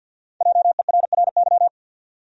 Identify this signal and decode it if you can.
Morse code, heard as a keyed tone, about 47 wpm, 700 Hz. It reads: JERRY
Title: USA Common Names - 50wpm